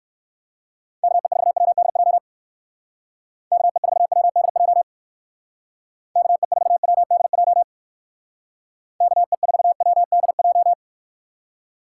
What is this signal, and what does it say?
Morse code, heard as a keyed tone, about 54 wpm, 700 Hz. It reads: KE4WDJ KE4WDJ KE4WDJ KE4WDJ
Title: Speed Racing - USA Call Signs 45wpm